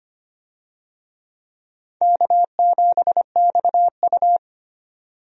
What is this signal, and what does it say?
Morse code, heard as a keyed tone, about 25 wpm, 700 Hz. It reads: K7XU
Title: USA Call Signs - 4 Characters - 25wpm